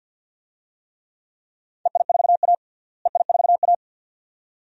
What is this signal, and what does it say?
Morse code, heard as a keyed tone, about 51 wpm, 700 Hz. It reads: EI4A EI4A
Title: DXCC-100 Call Signs - 4 Characters - Repeated 2x - 50wpm